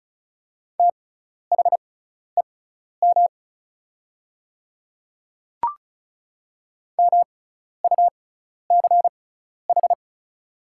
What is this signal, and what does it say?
Morse code, heard as a keyed tone, about 36 wpm, spaced as for 12 wpm, 690 Hz. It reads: THEM MUCH
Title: 6x Character Spacing - Top 300 Words - 35wpm